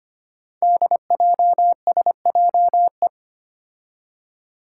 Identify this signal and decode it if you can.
Morse code, heard as a keyed tone, about 25 wpm, 700 Hz. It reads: DJSJE
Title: Five Challenging Letter-Number Combinations 25wpm